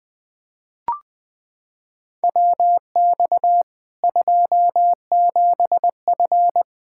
Morse code, heard as a keyed tone, about 20 wpm, 700 Hz. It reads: WX 27F
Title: Practice QSO - Exchange 2 - Info - 20wpm